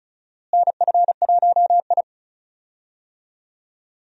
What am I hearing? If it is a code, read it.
Morse code, heard as a keyed tone, about 36 wpm, 700 Hz. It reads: NF1I
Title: USA Call Signs - 4 Characters - Repeated 2x - 35wpm